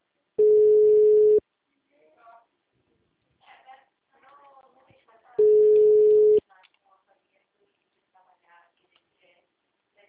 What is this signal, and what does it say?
Ringback tone (Europe, 425 Hz), with speech.
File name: out- -2008- - - .26169